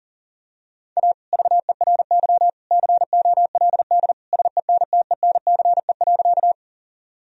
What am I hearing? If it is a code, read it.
Morse code, heard as a keyed tone, about 40 wpm, 700 Hz. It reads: A VERY COLD SENTENCE.
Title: Sentences from Top 1000 Words - Review Part 1 - 40wpm